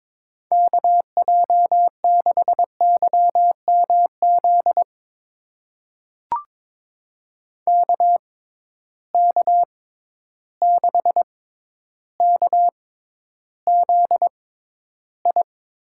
Morse code, heard as a keyed tone, about 22 wpm, spaced as for 11 wpm, 700 Hz. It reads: KJ6YMZ KK6KZI
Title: USA Call Signs - 6 Characters - 6x Character Spacing - 22wpm